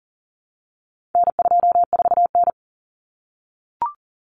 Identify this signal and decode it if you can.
Morse code, heard as a keyed tone, about 40 wpm, 700 Hz. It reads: N24N
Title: Common Cut Numbers - 4-Digit 40wpm